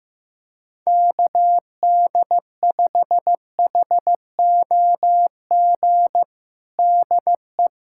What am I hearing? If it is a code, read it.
Morse code, heard as a keyed tone, about 16 wpm, 700 Hz. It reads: KD5HOG DE
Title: USA Callsign DE Callsign - 6 Characters - 15wpm